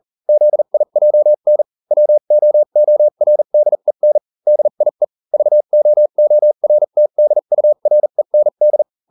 Morse code, heard as a keyed tone, about 40 wpm, spaced as for 35 wpm, 600 Hz. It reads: ZIJN WOORDEN DIE VOORTDUREND